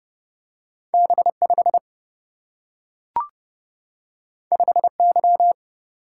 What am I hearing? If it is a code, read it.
Morse code, heard as a keyed tone, about 31 wpm, 700 Hz. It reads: B5 5Y
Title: Single-Letter Single-Number 30wpm